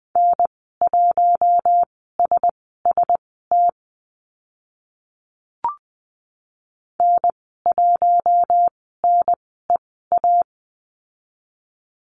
Morse code, heard as a keyed tone, about 20 wpm, spaced as for 15 wpm, 700 Hz. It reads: N1SST N1NEA
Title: 042 Lesson 8g - Number 1 - Call Signs 20wpm